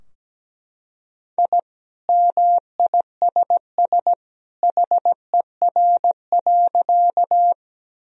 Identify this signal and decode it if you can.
Morse code, heard as a keyed tone, about 17 wpm, 700 Hz. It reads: I MISS HER.
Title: Sentences from Top 1000 Words - Review Part 3 - 17wpm